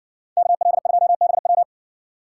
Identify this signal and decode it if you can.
Morse code, heard as a keyed tone, about 55 wpm, 700 Hz. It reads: KL2BW